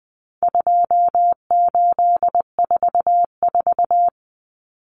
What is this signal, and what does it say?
Morse code, heard as a keyed tone, about 20 wpm, 700 Hz. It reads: 2844
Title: Numbers - 4-Digit 20wpm